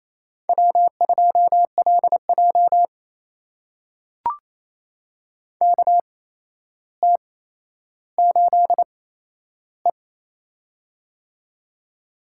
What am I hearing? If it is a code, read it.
Morse code, heard as a keyed tone, about 28 wpm, spaced as for 12 wpm, 700 Hz. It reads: W2LJ KT8E
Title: USA Call Signs - 4 Characters - 8x Character Spacing - 28wpm